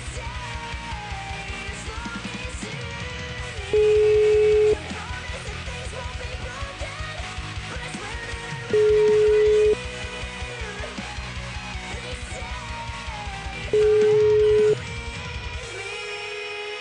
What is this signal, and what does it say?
Ringback tone (Europe, 425 Hz).